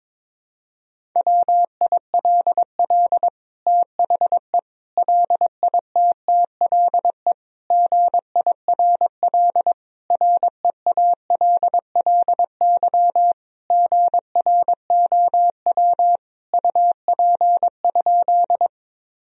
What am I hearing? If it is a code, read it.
Morse code, heard as a keyed tone, about 22 wpm, 700 Hz. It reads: WILL THE LITTLE GIRL REALLY GROW UP?
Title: Sentences from Top 1000 Words - Review Part 1 - 22wpm